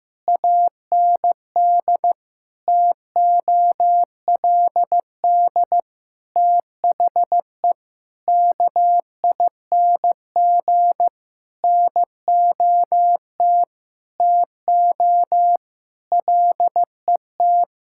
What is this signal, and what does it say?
Morse code, heard as a keyed tone, about 15 wpm, 700 Hz. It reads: AND TOLD THE KING NOT TO LET
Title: Sentences from Top 1000 Words - Review Part 6 - 15wpm